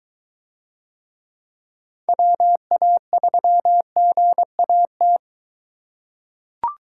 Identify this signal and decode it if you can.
Morse code, heard as a keyed tone, about 23 wpm, 700 Hz. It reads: WA3GAT